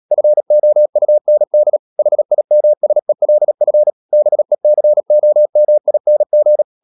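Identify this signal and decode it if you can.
Morse code, heard as a keyed tone, about 37 wpm, 600 Hz. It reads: FOUND HIMSELF BECOMING
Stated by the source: Morse code practice at 35–39 wpm sending English text of Ken Follett's novel Never, from CW med Gnister.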